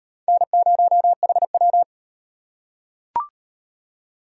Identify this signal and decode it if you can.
Morse code, heard as a keyed tone, about 38 wpm, 700 Hz. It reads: N0HW